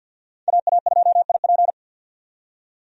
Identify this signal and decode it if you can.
Morse code, heard as a keyed tone, about 52 wpm, 700 Hz. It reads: AA2IP
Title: USA Call Signs - 5 Characters - 50wpm